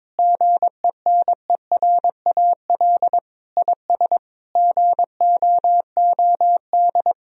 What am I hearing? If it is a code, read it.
Morse code, heard as a keyed tone, about 22 wpm, 700 Hz. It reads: GENERAL IS GOOD
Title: Sentences from Top 1000 Words - Review Part 3 - 22wpm